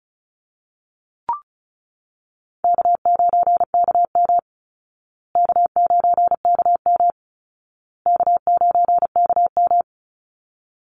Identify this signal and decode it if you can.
Morse code, heard as a keyed tone, about 34 wpm, 700 Hz. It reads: K9KM K9KM K9KM
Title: Easy Call Signs Repeated 3x - 35wpm